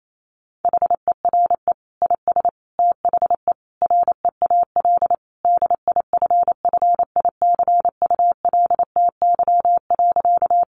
Morse code, heard as a keyed tone, about 28 wpm, 700 Hz. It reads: HERE IS THE REAL DIFFICULTY.
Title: Sentences from Top 1500 Words 28wpm